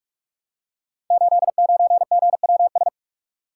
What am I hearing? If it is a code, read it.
Morse code, heard as a keyed tone, about 44 wpm, 700 Hz. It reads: 89GWS
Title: Code Groups - 5 Letter:Numbers - 45wpm